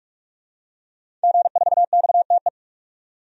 Morse code, heard as a keyed tone, about 46 wpm, 700 Hz. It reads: G4XTE
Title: DXCC-100 Call Signs - 5 Characters - 45wpm